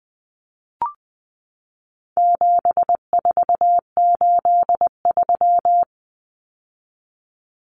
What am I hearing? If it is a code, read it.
Morse code, heard as a keyed tone, about 20 wpm, 700 Hz. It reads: 7483
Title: Common Cut Numbers - 4-Digit 20wpm